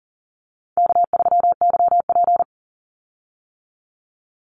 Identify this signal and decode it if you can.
Morse code, heard as a keyed tone, about 40 wpm, 700 Hz. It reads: K3YP
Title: Call Signs - 40wpm